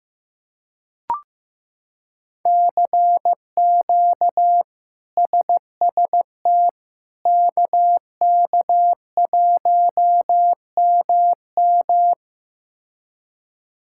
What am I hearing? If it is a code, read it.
Morse code, heard as a keyed tone, about 15 wpm, 700 Hz. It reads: CQ SST KK1MM
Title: K1USN SST QSO Practice - 15wpm